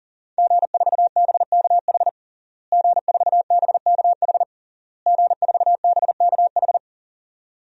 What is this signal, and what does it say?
Morse code, heard as a keyed tone, about 40 wpm, 700 Hz. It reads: G4BKH G4BKH G4BKH
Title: DXCC-100 Call Signs - 5 Characters - Repeated 3x - 40wpm